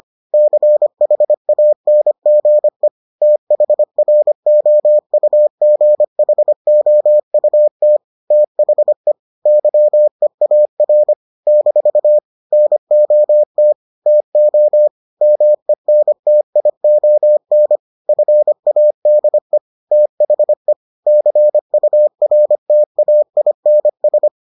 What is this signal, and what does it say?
Morse code, heard as a keyed tone, about 25 wpm, 600 Hz. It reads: CHANGE THROUGHOUT THE YEAR - NOT TO MENTION FADE THE CURTAINS